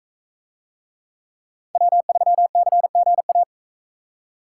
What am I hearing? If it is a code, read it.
Morse code, heard as a keyed tone, about 44 wpm, 700 Hz. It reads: W3CGA